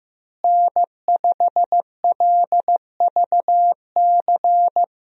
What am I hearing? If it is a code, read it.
Morse code, heard as a keyed tone, about 15 wpm, 700 Hz. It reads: N5LVC
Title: USA Call Signs - 5 Characters - Repeated 2x - 15wpm